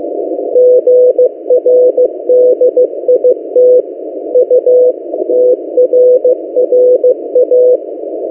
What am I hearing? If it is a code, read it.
Morse code, heard as a keyed tone, about 15 wpm, 500 Hz. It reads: GRDIT UTRRA
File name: M08a- -1355utc-8096khz